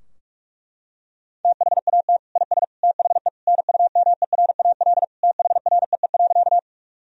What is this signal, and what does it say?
Morse code, heard as a keyed tone, about 46 wpm, 700 Hz. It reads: THAT IS THE NUMERAL THREE.
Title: Sentences from Top 1000 Words - Review Part 2 - 45wpm